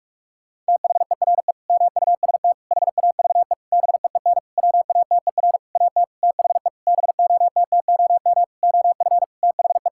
Morse code, heard as a keyed tone, about 46 wpm, 700 Hz. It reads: THERE MUST HAVE BEEN WATER AT THE BOTTOM OF THE
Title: Sentences from Top 1000 Words - Review Part 6 - 45wpm